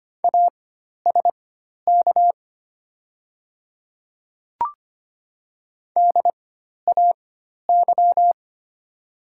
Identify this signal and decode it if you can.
Morse code, heard as a keyed tone, about 25 wpm, spaced as for 12 wpm, 700 Hz. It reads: ASK DAY
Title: 4x Character Spacing - Top 200 Words - 25wpm